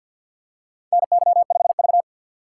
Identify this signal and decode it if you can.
Morse code, heard as a keyed tone, about 48 wpm, 700 Hz. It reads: NY5V
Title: USA Call Signs - 4 Characters - 2x Character Spacing - 50wpm